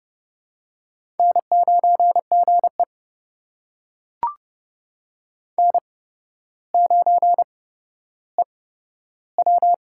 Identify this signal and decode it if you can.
Morse code, heard as a keyed tone, about 29 wpm, spaced as for 13 wpm, 700 Hz. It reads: N9GE N9EW
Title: USA Call Signs - 4 Characters - 8x Character Spacing - 30wpm